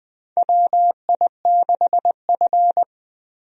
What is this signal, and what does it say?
Morse code, heard as a keyed tone, about 21 wpm, 700 Hz. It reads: WI6F
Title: USA Call Signs - 4 Characters - Repeated 2x - 20wpm